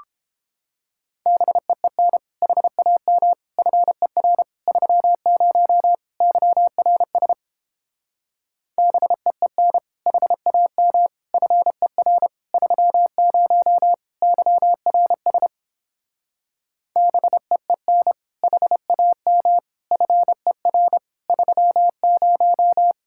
Morse code, heard as a keyed tone, about 29 wpm, 700 Hz. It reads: BEEN HAM FER 30 YRS BEEN HAM FER 30 YRS BEEN HAM FER 30